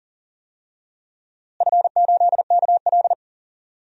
Morse code, heard as a keyed tone, about 37 wpm, 700 Hz. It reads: F8KL
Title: DXCC-100 Call Signs - 4 Characters - Repeated 3x - 40wpm